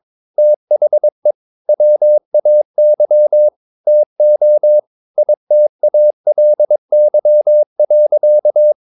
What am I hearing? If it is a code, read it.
Morse code, heard as a keyed tone, about 22 wpm, 600 Hz. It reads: THE WAY TO ITALY.
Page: Romanen Never af Ken Follet på engelsk.